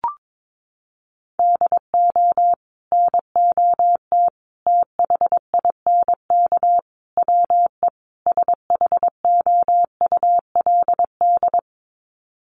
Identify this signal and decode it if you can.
Morse code, heard as a keyed tone, about 22 wpm, 700 Hz. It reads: DO NOT THINK WE SHOULD
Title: Sets of 5 Words Encore 22wpm